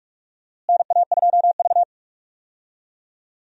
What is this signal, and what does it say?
Morse code, heard as a keyed tone, about 46 wpm, 700 Hz. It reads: NA2V
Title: USA Call Signs - 4 Characters - Repeated 3x - 45wpm